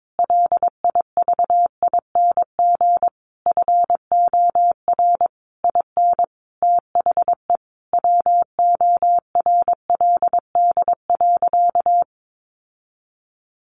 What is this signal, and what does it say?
Morse code, heard as a keyed tone, about 22 wpm, 700 Hz. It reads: LIVING FOR IN THE WORLD.
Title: Sentences from Top 1100 Words 22wpm